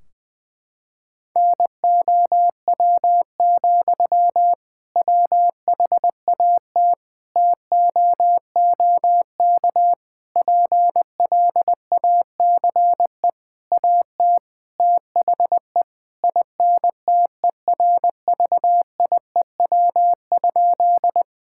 Morse code, heard as a keyed tone, about 21 wpm, 700 Hz. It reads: NOW, WHAT TOOK PLACE AT THE INTERVIEW?
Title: Sentences from Top 1600 Words 20wpm